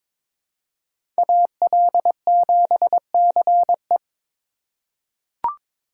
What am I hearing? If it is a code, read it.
Morse code, heard as a keyed tone, about 22 wpm, 700 Hz. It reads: AL7CE
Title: USA Call Signs - 5 Characters - 8x Character Spacing - 22wpm